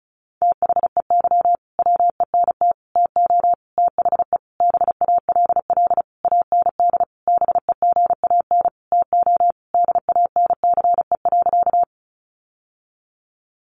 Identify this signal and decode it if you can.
Morse code, heard as a keyed tone, about 34 wpm, 700 Hz. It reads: THEY WENT TO THE BALL AND BEGAN TO DANCE.
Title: Sentences from Top 900 Words 35wpm